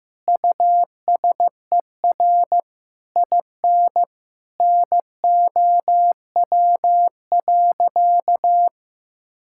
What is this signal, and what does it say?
Morse code, heard as a keyed tone, about 15 wpm, 700 Hz. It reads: USER IN NOW.
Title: Sentences from Top 1000 Words - Review Part 5 - 15wpm